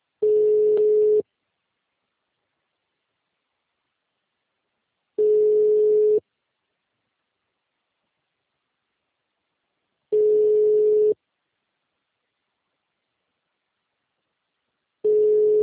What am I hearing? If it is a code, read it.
Ringback tone (Europe, 425 Hz).